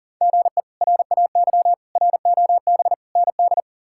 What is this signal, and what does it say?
Morse code, heard as a keyed tone, about 40 wpm, 700 Hz. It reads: GE RAY ROB ND
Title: K1USN SST QSO Practice - 40wpm